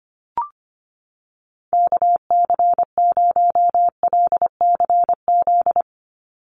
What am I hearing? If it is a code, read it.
Morse code, heard as a keyed tone, about 25 wpm, 700 Hz. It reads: KC0LCZ
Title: USA Call Signs 25wpm